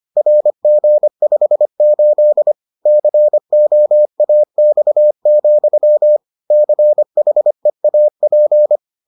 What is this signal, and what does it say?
Morse code, heard as a keyed tone, about 25 wpm, 600 Hz. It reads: RG58 COAX, CHEAP